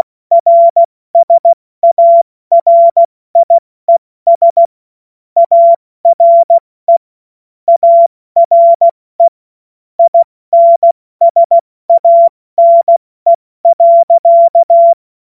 Morse code, heard as a keyed tone, about 16 wpm, 680 Hz. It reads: RSARIES ARE ARE INSANE.
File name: Morsekurs / Witze / Witz 23 Januar 2024 70 F80